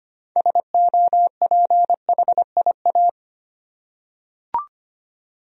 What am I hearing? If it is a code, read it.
Morse code, heard as a keyed tone, about 25 wpm, 700 Hz. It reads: SOPHIA
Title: USA Common Names - 25wpm